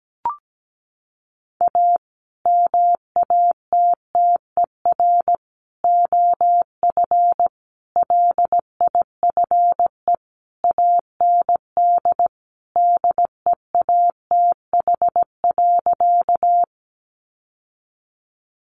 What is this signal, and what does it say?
Morse code, heard as a keyed tone, about 17 wpm, 700 Hz. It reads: A MATTER OF LIFE AND DEATH.
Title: Sentences from Top 1100 Words 17wpm